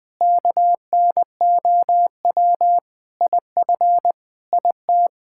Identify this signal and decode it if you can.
Morse code, heard as a keyed tone, about 20 wpm, 700 Hz. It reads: KNOW IF IT
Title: Sentences with 5-Word Phrases - Spoken Beginning - 20wpm